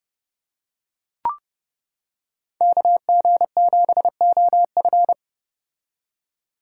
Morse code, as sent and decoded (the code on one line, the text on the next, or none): -.- --. --... --- ..-.
KG7OF